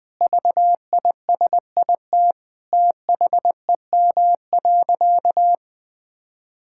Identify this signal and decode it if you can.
Morse code, heard as a keyed tone, about 20 wpm, 700 Hz. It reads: VISIT THEM.
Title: Sentences from Top 1000 Words - Review Part 6 - 20wpm